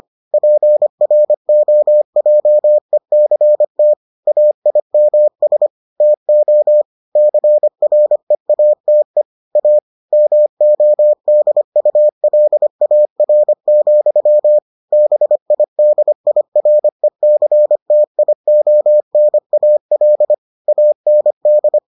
Morse code, heard as a keyed tone, about 25 wpm, 600 Hz. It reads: PROJECT AIMS TO CREATE A MODULAR, BIDIRECTIONAL AND